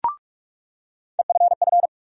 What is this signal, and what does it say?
Morse code, heard as a keyed tone, about 45 wpm, 700 Hz. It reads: EFF
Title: USA Call Sign Postfixes - 3 Letters 45wpm